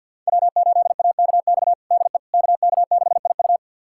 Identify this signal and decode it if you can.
Morse code, heard as a keyed tone, about 51 wpm, 700 Hz. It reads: W8AKX DE KK6IU
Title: USA Callsign DE Callsign - 5 Characters - 50wpm